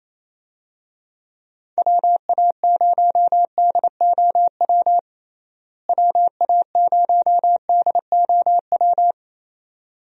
Morse code, heard as a keyed tone, about 27 wpm, 700 Hz. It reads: WA0DOW WA0DOW
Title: USA Call Signs - 6 Characters - Repeated 2x - 28wpm